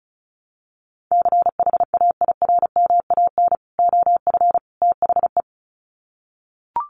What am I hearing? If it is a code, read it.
Morse code, heard as a keyed tone, about 34 wpm, 700 Hz. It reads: CHAIRMAN OF THE
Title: Sets of 3 Words 35wpm